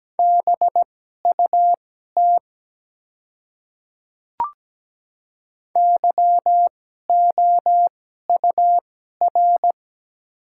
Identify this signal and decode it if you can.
Morse code, heard as a keyed tone, about 17 wpm, spaced as for 12 wpm, 700 Hz. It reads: BUT YOUR
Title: 2x Character Spacing - Top 200 Words - 17wpm